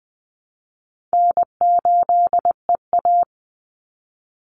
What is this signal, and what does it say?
Morse code, heard as a keyed tone, about 20 wpm, 700 Hz. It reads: N8EA
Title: Call Signs - 20wpm